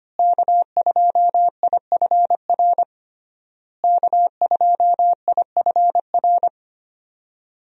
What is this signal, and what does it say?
Morse code, heard as a keyed tone, about 25 wpm, 700 Hz. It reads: K2IFR K2IFR
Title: USA Call Signs - 5 Characters - Repeated 3x - 25wpm